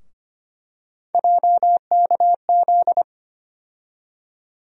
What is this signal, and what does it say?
Morse code, heard as a keyed tone, about 25 wpm, 700 Hz. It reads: JKZ
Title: USA Call Sign Postfixes - 3 Letters 25wpm